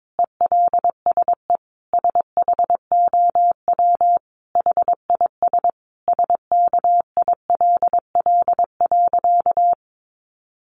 Morse code, heard as a keyed tone, about 22 wpm, 700 Hz. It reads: ELSE SHOW HIS SKILL.
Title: Sentences from Top 1200 Words 22wpm